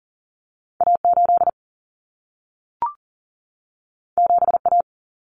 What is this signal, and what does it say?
Morse code, heard as a keyed tone, about 40 wpm, 700 Hz. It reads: A8 7A
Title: Common Cut Numbers - 2-Digit 40wpm